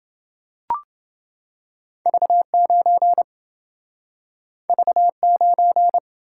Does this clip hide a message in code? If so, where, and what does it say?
Yes: Morse code from 2.1 to 6.0 s, heard as a keyed tone, about 27 wpm, 700 Hz. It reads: V9 V9